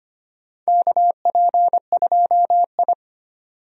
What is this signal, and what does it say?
Morse code, heard as a keyed tone, about 25 wpm, 700 Hz. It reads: KP2I
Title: DXCC-100 Call Signs - 4 Characters - 25wpm